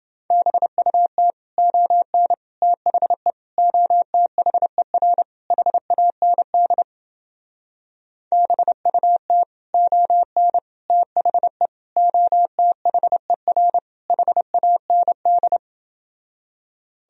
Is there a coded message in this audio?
Yes: Morse code from 0.3 to 15.6 s, heard as a keyed tone, about 29 wpm, 700 Hz. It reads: BUT ON THE OTHER HAND BUT ON THE OTHER HAND